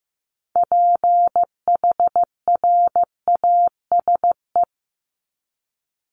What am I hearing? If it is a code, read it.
Morse code, heard as a keyed tone, about 15 wpm, 700 Hz. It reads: PHRASE
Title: Top 1501-1750 Words 15wpm